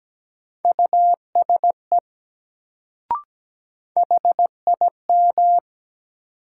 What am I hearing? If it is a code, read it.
Morse code, heard as a keyed tone, about 17 wpm, 700 Hz. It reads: USE HIM
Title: Sentences from Top 100 Words - New Word Drill - 17wpm